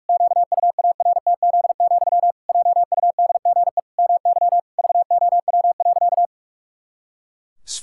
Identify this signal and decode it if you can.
Morse code, heard as a keyed tone, about 46 wpm, 700 Hz. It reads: QUARTZ, JUDGE MY VOW.
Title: Pangram Challenge v2 - 45wpm